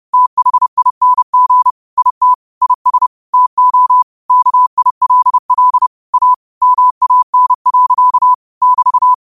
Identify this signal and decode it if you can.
Morse code, heard as a keyed tone, about 30 wpm, 1000 Hz. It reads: THING IT IS TO KILL A MAN. <BT>